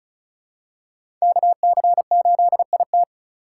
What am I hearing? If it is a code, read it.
Morse code, heard as a keyed tone, about 36 wpm, 700 Hz. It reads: KC8IT
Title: USA Call Signs - 5 Characters - 2x Character Spacing - 35wpm